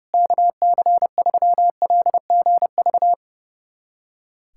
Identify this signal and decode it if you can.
Morse code, heard as a keyed tone, about 32 wpm, 700 Hz. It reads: KC3LGV
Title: USA Callsign DE Callsign - 6 Characters - 30wpm